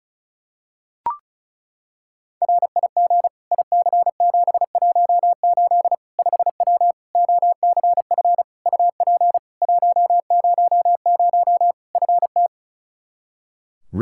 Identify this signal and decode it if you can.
Morse code, heard as a keyed tone, about 36 wpm, 700 Hz. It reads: RIG IC718 5W OCF UP 100 FT
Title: K1USN SST QSO Practice 35wpm